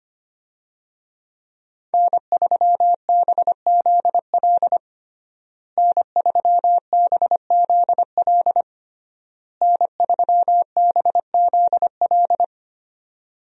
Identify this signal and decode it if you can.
Morse code, heard as a keyed tone, about 25 wpm, 700 Hz. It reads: N3BZL N3BZL N3BZL
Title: USA Call Signs - 5 Characters - Repeated 3x - 25wpm